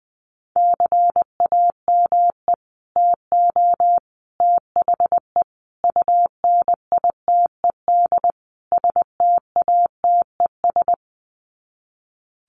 Morse code, heard as a keyed tone, about 20 wpm, 700 Hz. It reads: CAME TO THE UNITED STATES
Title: Sets of 5 Words Encore2 20wpm